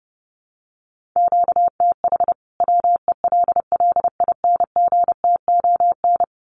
Morse code, heard as a keyed tone, about 30 wpm, 700 Hz. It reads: QTH WELLINGTON
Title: Common QSO Phrases 30wpm